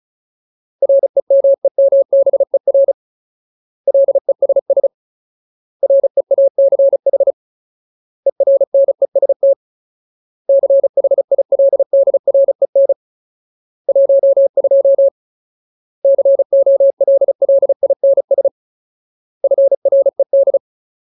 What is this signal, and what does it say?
Morse code, heard as a keyed tone, about 34 wpm, 550 Hz. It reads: REMEMBER LESS REACH ERNEST CHILDREN 12 COLLINS FRED
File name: Ultimate Word List / 35WPM Ultimate